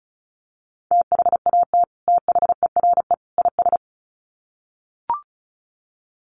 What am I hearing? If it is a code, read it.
Morse code, heard as a keyed tone, about 34 wpm, 700 Hz. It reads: THAT THERE IS
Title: Sets of 3 Words 35wpm